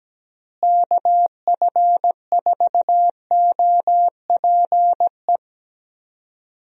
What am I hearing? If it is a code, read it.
Morse code, heard as a keyed tone, about 17 wpm, 700 Hz. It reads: KF4OPE
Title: USA Call Signs - 6 Characters - Repeated 2x - 17wpm